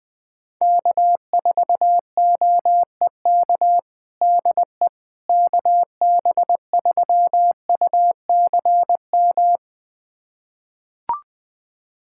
Morse code, heard as a keyed tone, about 20 wpm, 700 Hz. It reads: K4OEK DE KB3UCM
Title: USA Callsign DE Callsign 20wpm